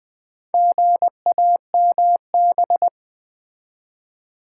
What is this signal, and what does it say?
Morse code, heard as a keyed tone, about 20 wpm, 700 Hz. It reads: GAMB
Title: Build A Word - 2 Letters At A Time - 8-Letter Words - 20wpm